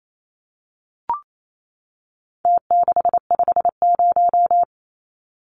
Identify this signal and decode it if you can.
Morse code, heard as a keyed tone, about 28 wpm, 700 Hz. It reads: T650
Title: Numbers - Contest Style - 4-Digit 28wpm